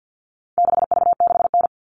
Morse code, heard as a keyed tone, about 50 wpm, 700 Hz. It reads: 6V6N
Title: All Cut Numbers - 4-Digit 50wpm